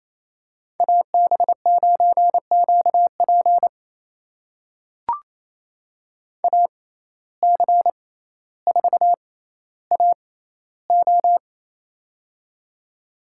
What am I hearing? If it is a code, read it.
Morse code, heard as a keyed tone, about 28 wpm, spaced as for 14 wpm, 700 Hz. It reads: AB9QP AC4AO
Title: USA Call Signs - 5 Characters - 6x Character Spacing - 28wpm